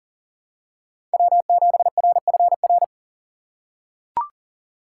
Morse code, heard as a keyed tone, about 40 wpm, 700 Hz. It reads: W7RFR
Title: USA Call Signs - 5 Characters - 4x Character Spacing - 40wpm